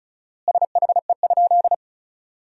Morse code, heard as a keyed tone, about 36 wpm, 700 Hz. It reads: SHE?
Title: 4x Word-Spacing - Sentences from Top 100 Words - 35wpm